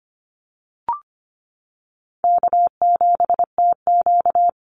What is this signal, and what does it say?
Morse code, heard as a keyed tone, about 25 wpm, 700 Hz. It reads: K7TQ
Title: NA SOTA Chaser Callsigns 25wpm